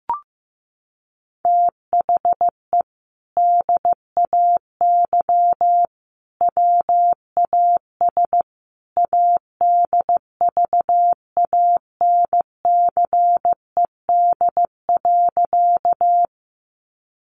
Morse code, heard as a keyed tone, about 14 wpm, 700 Hz. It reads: THE DAY WAS ADVANCED.
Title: Sentences from Top 1200 Words 15wpm